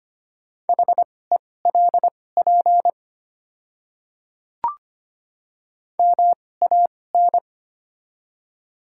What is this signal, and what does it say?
Morse code, heard as a keyed tone, about 25 wpm, spaced as for 18 wpm, 700 Hz. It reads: HELP MAN
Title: 2x Character Spacing - Top 300 Words - 25wpm